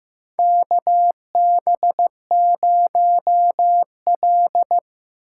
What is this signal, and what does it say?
Morse code, heard as a keyed tone, about 15 wpm, 700 Hz. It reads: KB0L
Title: USA Call Signs - 4 Characters - 4x Character Spacing - 15wpm